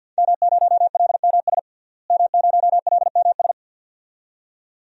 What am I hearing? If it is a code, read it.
Morse code, heard as a keyed tone, about 52 wpm, 700 Hz. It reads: M0LMS M0LMS
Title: DXCC-100 Call Signs - 5 Characters - Repeated 2x - 50wpm